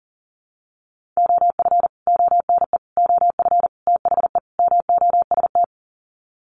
Morse code, heard as a keyed tone, about 40 wpm, 700 Hz. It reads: OF ONE OF THE MOST
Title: Sets of 5 Words Encore2 40wpm